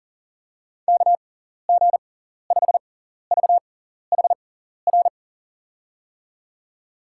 Morse code, heard as a keyed tone, about 40 wpm, spaced as for 14 wpm, 700 Hz. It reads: KG5VHR
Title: USA Call Signs - 6 Characters - 6x Character Spacing - 40wpm